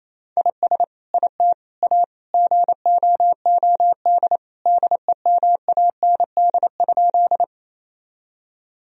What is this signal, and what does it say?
Morse code, heard as a keyed tone, about 28 wpm, 700 Hz. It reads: IS IT A GOOD DEMAND?
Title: Sentences from Top 1000 Words - Review Part 4 - 28wpm